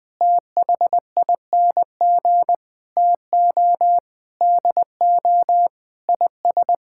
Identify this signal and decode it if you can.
Morse code, heard as a keyed tone, about 20 wpm, 700 Hz. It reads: THING TO DO IS
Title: Sentences with 7-Word Phrases - 20wpm